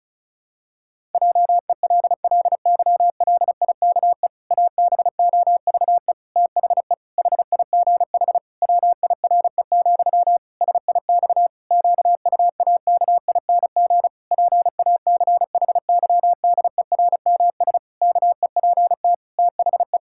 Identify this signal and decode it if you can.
Morse code, heard as a keyed tone, about 34 wpm, 700 Hz. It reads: JELLYLIKE ABOVE THE HIGH WIRE, SIX QUAKING PACHYDERMS KEPT THE
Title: Pangram Challenge v2 - 35wpm